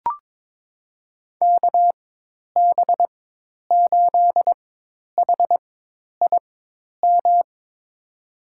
Morse code, heard as a keyed tone, about 22 wpm, spaced as for 10 wpm, 700 Hz. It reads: KB8HIM
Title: USA Call Signs - 4x Character Spacing - 22wpm